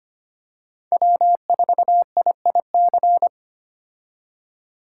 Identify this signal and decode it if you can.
Morse code, heard as a keyed tone, about 25 wpm, 700 Hz. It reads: W4IIC
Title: K1USN SST QSO Practice - 25wpm